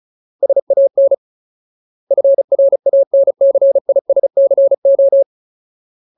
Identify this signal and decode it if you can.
Morse code, heard as a keyed tone, about 34 wpm, 550 Hz. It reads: SAN FRANCISCO
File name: Ultimate Word List / 35WPM Ultimate